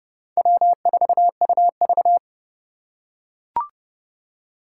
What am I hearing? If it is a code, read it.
Morse code, heard as a keyed tone, about 29 wpm, 700 Hz. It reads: W4UV